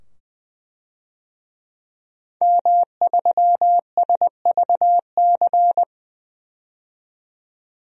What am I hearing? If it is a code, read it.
Morse code, heard as a keyed tone, about 20 wpm, 700 Hz. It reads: M3SVC
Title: DXCC-100 Call Signs - 5 Characters - 20wpm